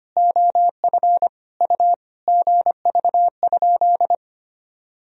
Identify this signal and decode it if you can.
Morse code, heard as a keyed tone, about 25 wpm, 700 Hz. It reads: OF U GV?
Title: Sentences from Top 200 Words and Abbreviations 25wpm